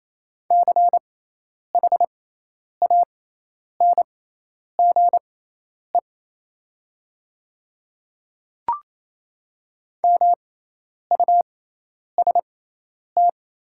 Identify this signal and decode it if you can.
Morse code, heard as a keyed tone, about 28 wpm, spaced as for 10 wpm, 690 Hz. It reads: CHANGE MUST
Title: 6x Character Spacing - Top 200 Words - 28wpm